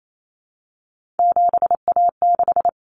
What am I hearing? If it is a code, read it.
Morse code, heard as a keyed tone, about 28 wpm, 700 Hz. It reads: 7A6
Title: Common Cut Numbers - 3-Digit 28wpm